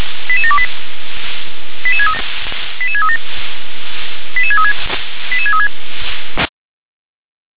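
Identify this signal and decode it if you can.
Five-tone selective call (ZVEI), heard as quick runs of five tones, tones of about 70 ms. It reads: 88728 88637 88636 88636
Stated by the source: ZVEI five-tone call 88783 received by SDR on channel 357.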